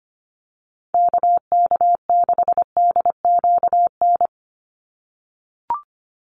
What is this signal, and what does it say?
Morse code, heard as a keyed tone, about 25 wpm, 700 Hz. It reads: KK6DQN
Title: USA Call Signs 25wpm